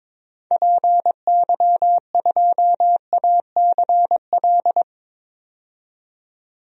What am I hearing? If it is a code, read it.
Morse code, heard as a keyed tone, about 23 wpm, 700 Hz. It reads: PY2ACL
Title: DXCC-100 Call Signs - 6 Characters - 22wpm